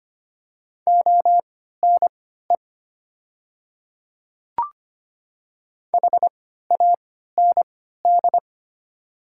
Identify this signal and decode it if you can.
Morse code, heard as a keyed tone, about 25 wpm, spaced as for 14 wpm, 700 Hz. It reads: ONE HAND
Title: 3x Character Spacing - Top 500 Words - 25wpm